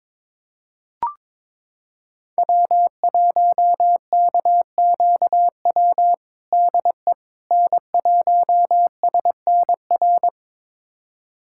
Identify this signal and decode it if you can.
Morse code, heard as a keyed tone, about 21 wpm, 700 Hz. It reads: W1KQW DE N1SNR
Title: USA Callsign DE Callsign - 5 Characters - 22wpm